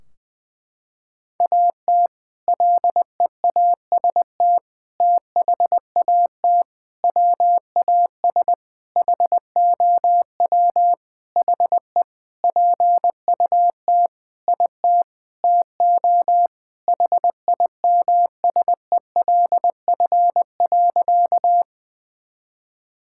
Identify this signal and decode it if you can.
Morse code, heard as a keyed tone, about 21 wpm, 700 Hz. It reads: AT LEAST THAT WAS HOW HE PUT IT TO HIMSELF.
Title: Sentences from Top 1600 Words 20wpm